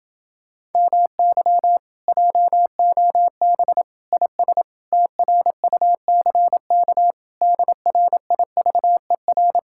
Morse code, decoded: MY JOB IS TRUCK DRIVER

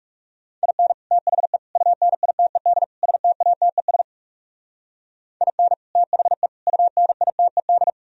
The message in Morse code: IN THE UNITED STATES IN THE UNITED

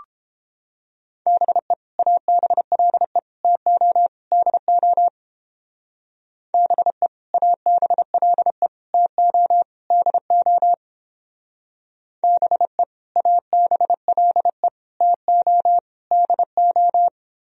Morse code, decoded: BE ABLE TO DO BE ABLE TO DO BE ABLE TO DO

